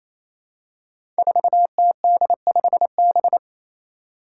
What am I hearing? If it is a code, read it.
Morse code, heard as a keyed tone, about 28 wpm, 700 Hz. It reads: 4TD5B
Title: Five Challenging Letter-Number Combinations 28wpm